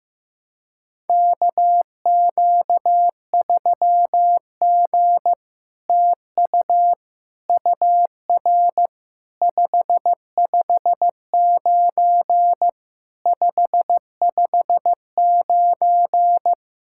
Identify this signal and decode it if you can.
Morse code, heard as a keyed tone, about 15 wpm, 700 Hz. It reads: KQ3G TU UR 559 559
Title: SOTA QSO Practice 15wpm